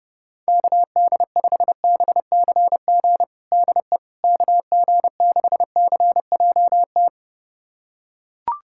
Morse code, heard as a keyed tone, about 32 wpm, 700 Hz. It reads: KD5BCG DE KG6CJT
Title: USA Callsign DE Callsign 30wpm